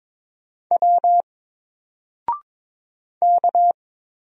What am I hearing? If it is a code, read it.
Morse code, heard as a keyed tone, about 22 wpm, 700 Hz. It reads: W K
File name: Single-Letter-Number-22wpm